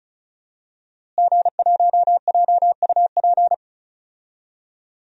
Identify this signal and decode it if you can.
Morse code, heard as a keyed tone, about 36 wpm, 700 Hz. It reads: G1JUP
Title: DXCC-100 Call Signs - 5 Characters - Repeated 3x - 35wpm